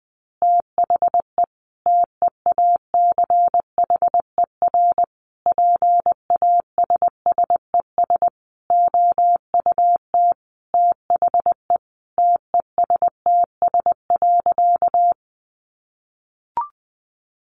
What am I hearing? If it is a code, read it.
Morse code, heard as a keyed tone, about 20 wpm, 700 Hz. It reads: THE TEACHER PASSES OUT THE TESTS.
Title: Additional Easy Phrases 20wpm